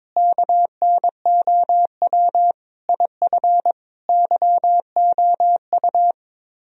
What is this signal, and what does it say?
Morse code, heard as a keyed tone, about 22 wpm, 700 Hz. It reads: KNOW IF YOU
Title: Speed-Racing - Sets of 5 Words 22wpm